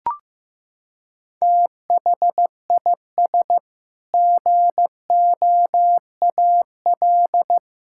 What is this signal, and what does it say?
Morse code, heard as a keyed tone, about 15 wpm, 700 Hz. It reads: THIS GOAL
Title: Sentences from Top 1000 Words - Review Part 5 - 15wpm